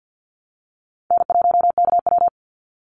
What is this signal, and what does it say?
Morse code, heard as a keyed tone, about 50 wpm, 700 Hz. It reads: N1KW
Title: Easy Call Signs Repeated 2x - 50wpm